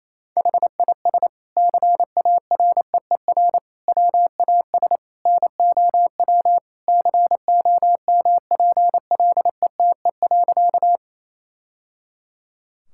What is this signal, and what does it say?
Morse code, heard as a keyed tone, about 28 wpm, 700 Hz. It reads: HIS CAREER WAS NOW COMPLETE.
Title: Sentences from Top 1000 Words - Review Part 6 - 28wpm